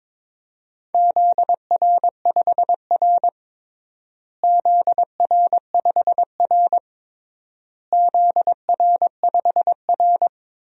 Morse code, heard as a keyed tone, about 22 wpm, 700 Hz. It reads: ZR5R ZR5R ZR5R
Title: DXCC-100 Call Signs - 4 Characters - Repeated 3x - 22wpm